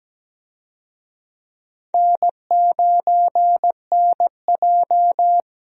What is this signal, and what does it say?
Morse code, heard as a keyed tone, about 17 wpm, 700 Hz. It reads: N9NJ
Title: USA Call Signs - 4 Characters - Repeated 2x - 17wpm